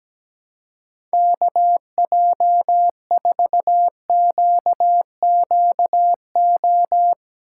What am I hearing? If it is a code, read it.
Morse code, heard as a keyed tone, about 17 wpm, 700 Hz. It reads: KJ4QQO
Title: USA Call Signs - 6 Characters - 4x Character Spacing - 17wpm